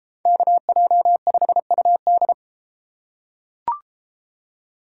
Morse code, heard as a keyed tone, about 32 wpm, 700 Hz. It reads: KJ5UD